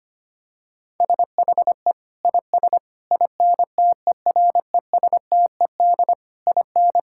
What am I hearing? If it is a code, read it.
Morse code, heard as a keyed tone, about 25 wpm, 700 Hz. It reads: SHE IS INTERESTED IN
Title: Sentences from Top 1000 Words - Review Part 6 - 25wpm